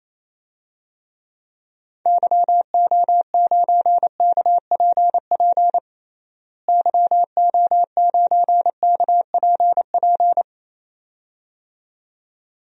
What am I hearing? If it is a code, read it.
Morse code, heard as a keyed tone, about 27 wpm, 700 Hz. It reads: YO9KPP YO9KPP
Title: DXCC-100 Call Signs - 6 Characters - Repeated 2x - 28wpm